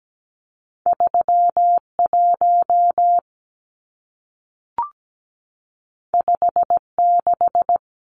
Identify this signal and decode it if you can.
Morse code, heard as a keyed tone, about 17 wpm, 700 Hz. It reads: 31 56
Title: Numbers - 2-Digit 17wpm